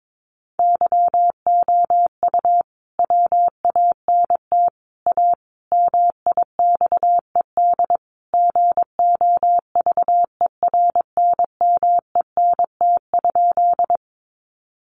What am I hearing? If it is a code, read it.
Morse code, heard as a keyed tone, about 22 wpm, 700 Hz. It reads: YOU WANT A MIXED GOVERNMENT?
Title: Sentences from Top 1500 Words 22wpm